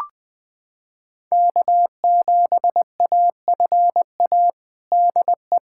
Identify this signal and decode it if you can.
Morse code, heard as a keyed tone, about 20 wpm, 700 Hz. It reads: K7AFA DE
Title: USA Callsign DE Callsign - 5 Characters - 20wpm